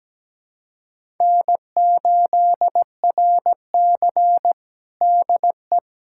Morse code, heard as a keyed tone, about 17 wpm, 700 Hz. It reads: N8RC DE
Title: USA Callsign DE Callsign - 4 Characters - 17wpm